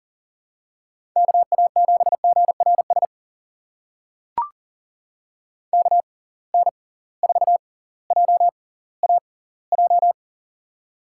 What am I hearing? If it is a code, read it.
Morse code, heard as a keyed tone, about 40 wpm, spaced as for 20 wpm, 700 Hz. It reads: KA7GRS KN4JAJ
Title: USA Call Signs - 6x Character Spacing - 40wpm